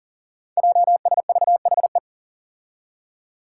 Morse code, heard as a keyed tone, about 40 wpm, 700 Hz. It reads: JSVHE
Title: Five Challenging Letter-Number Combinations 40wpm